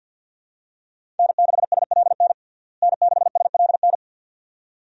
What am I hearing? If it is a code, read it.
Morse code, heard as a keyed tone, about 49 wpm, 700 Hz. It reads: N6SLN N6SLN
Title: USA Call Signs - 5 Characters - Repeated 2x - 50wpm